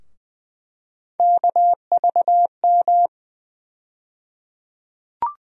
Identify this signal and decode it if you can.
Morse code, heard as a keyed tone, about 20 wpm, 700 Hz. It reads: KVM
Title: USA Call Sign Postfixes - 3 Letters 20wpm